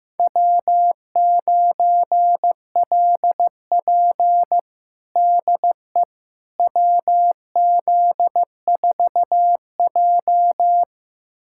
Morse code, heard as a keyed tone, about 15 wpm, 700 Hz. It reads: W9LP DE WZ4J
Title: USA Callsign DE Callsign - 4 Characters - 15wpm